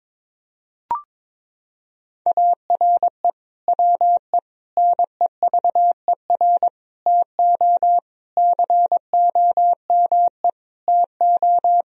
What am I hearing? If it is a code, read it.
Morse code, heard as a keyed tone, about 22 wpm, 700 Hz. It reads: ARE WE NEVER TO COME TO
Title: Sentences from Top 1000 Words - Review Part 3 - 22wpm